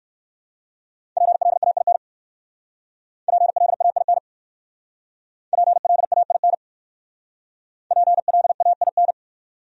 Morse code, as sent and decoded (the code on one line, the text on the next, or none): .--. .-.. .- .. -. / .--. .-.. .- .. -. / .--. .-.. .- .. -. / .--. .-.. .- .. -.
PLAIN PLAIN PLAIN PLAIN